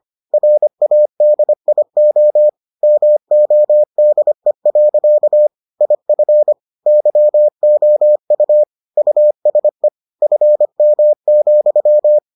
Morse code, heard as a keyed tone, about 25 wpm, 600 Hz. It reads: RADIO MODE. IF YOU USE FM,